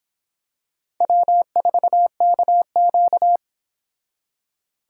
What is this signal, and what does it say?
Morse code, heard as a keyed tone, about 26 wpm, 700 Hz. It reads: W4KQ